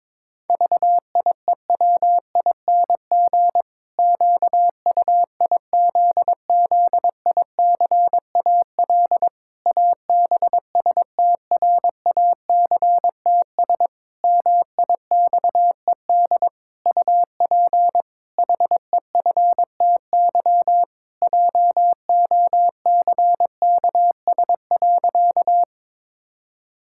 Morse code, heard as a keyed tone, about 22 wpm, 700 Hz. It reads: VIEWING QUIZZICAL ABSTRACTS MIXED UP HEFTY JOCKS.
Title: Pangram Challenge v2 - 22wpm